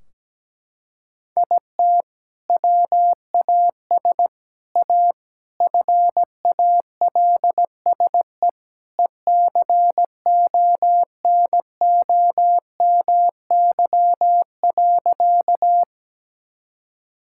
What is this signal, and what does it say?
Morse code, heard as a keyed tone, about 17 wpm, 700 Hz. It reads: IT WAS A FALSE ECONOMY.
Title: Sentences from Top 1600 Words 17wpm